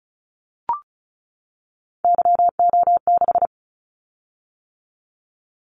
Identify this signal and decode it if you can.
Morse code, heard as a keyed tone, about 35 wpm, 700 Hz. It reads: YO6
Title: Callsign Prefixes for Top 25 DXCC - 35wpm